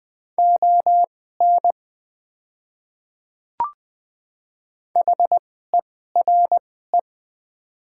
Morse code, heard as a keyed tone, about 20 wpm, spaced as for 15 wpm, 700 Hz. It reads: ON HERE
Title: 2x Character Spacing - Top 200 Words - 20wpm